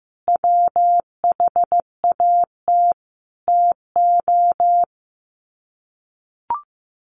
Morse code, heard as a keyed tone, about 15 wpm, 700 Hz. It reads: WHAT TO
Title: Sets of 5 Words 15wpm